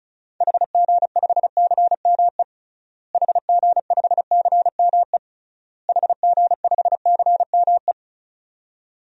Morse code, heard as a keyed tone, about 36 wpm, 700 Hz. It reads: HG5CME HG5CME HG5CME
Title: DXCC-100 Call Signs - 6 Characters - Repeated 3x - 35wpm